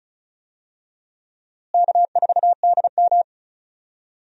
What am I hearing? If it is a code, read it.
Morse code, heard as a keyed tone, about 36 wpm, 700 Hz. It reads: K4DM
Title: USA Call Signs - 4 Characters - Repeated 2x - 35wpm